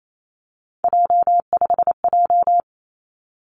Morse code, heard as a keyed tone, about 28 wpm, 700 Hz. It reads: J5J
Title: Three Challenging Letter-Number Combinations 28wpm